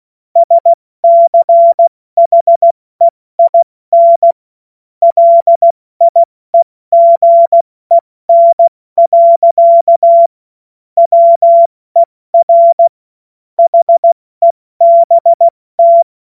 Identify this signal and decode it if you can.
Morse code, heard as a keyed tone, about 16 wpm, 680 Hz. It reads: SCHEIN LIEGEN. WER HEBT
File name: Morsekurs / Witze / Witz 03. Januar 2023 70 F80